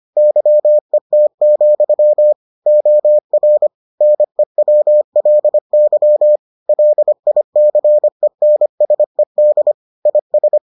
Morse code, heard as a keyed tone, about 25 wpm, 600 Hz. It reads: YET, OR NEWLY LICENSED IS